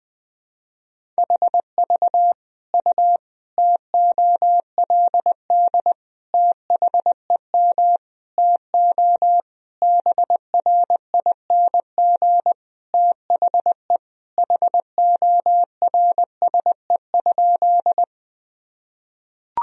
Morse code, heard as a keyed tone, about 21 wpm, 700 Hz. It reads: HV U TOLD THEM TO BRING THE HORSE?
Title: Sentences from Top 500 Words and Abbreviations 20wpm